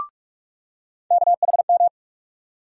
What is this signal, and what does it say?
Morse code, heard as a keyed tone, about 42 wpm, 700 Hz. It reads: KHM
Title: USA Call Sign Postfixes - 3 Letters 45wpm